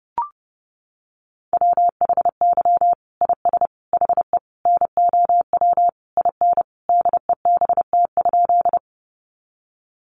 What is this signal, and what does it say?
Morse code, heard as a keyed tone, about 30 wpm, 700 Hz. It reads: WHY IS HE NOW IN DEBT?
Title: Sentences from Top 1300 Words 30wpm